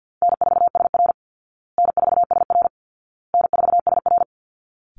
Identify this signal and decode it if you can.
Morse code, heard as a keyed tone, about 50 wpm, 700 Hz. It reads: N4SR N4SR N4SR
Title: Easy Call Signs Repeated 3x - 50wpm